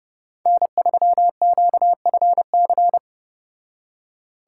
Morse code, heard as a keyed tone, about 35 wpm, 700 Hz. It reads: N3QFC